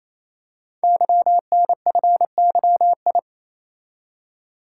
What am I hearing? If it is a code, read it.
Morse code, heard as a keyed tone, about 29 wpm, 700 Hz. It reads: YNFYI
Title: Code Groups - 5 Letters - 28wpm